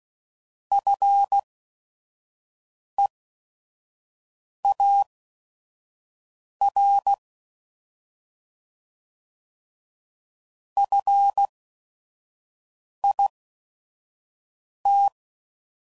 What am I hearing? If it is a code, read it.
Morse code, heard as a keyed tone, about 16 wpm, spaced as for 5 wpm, 800 Hz. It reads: FEAR FIT